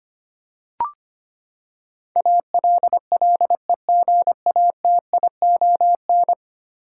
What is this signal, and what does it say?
Morse code, heard as a keyed tone, about 25 wpm, 700 Hz. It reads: ALLEGATION
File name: Ten Letter Words - 25wpm